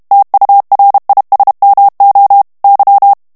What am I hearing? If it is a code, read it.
Morse code, heard as a keyed tone, about 32 wpm, 800 Hz. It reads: TURISMO Y
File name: ESPAGNOL TEXTE EN CLAIR 2 VITESSE 30 MOTS MINUTES